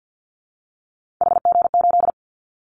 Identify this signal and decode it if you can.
Morse code, heard as a keyed tone, about 50 wpm, 700 Hz. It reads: HG8
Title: Callsign Prefixes for Top 50 DXCC - 50wpm